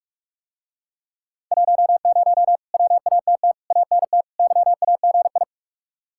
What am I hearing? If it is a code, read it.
Morse code, heard as a keyed tone, about 49 wpm, 700 Hz. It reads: 10 WATT ANT YAGI